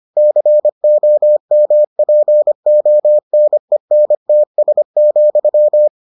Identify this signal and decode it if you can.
Morse code, heard as a keyed tone, about 25 wpm, 600 Hz. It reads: COMPONENTS,